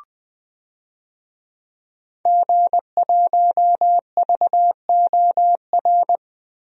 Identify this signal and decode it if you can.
Morse code, heard as a keyed tone, about 20 wpm, 700 Hz. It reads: G1VOR
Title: DXCC-100 Call Signs - 5 Characters - Repeated 2x - 20wpm